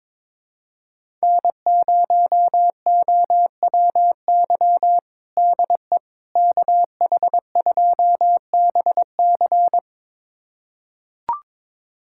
Morse code, heard as a keyed tone, about 22 wpm, 700 Hz. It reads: N0OWY DE KH2BC
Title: USA Callsign DE Callsign - 5 Characters - 22wpm